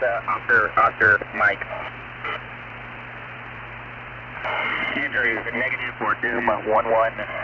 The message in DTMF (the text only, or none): none